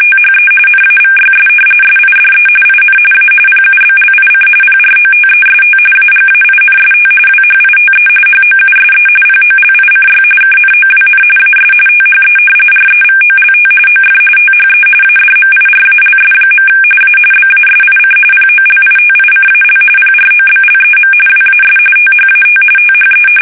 RTTY, about 45.45 baud. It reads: THE NAVYS SIDE IN A PUBLIC PROTEST AGAINST WHITE HOUSE POLICY KNOWN AS THE REVOLT OF THE ADMIRALS. A BASIC POLITICAL PROBLEM WAS THAT THE
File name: fsk850 / usnavy5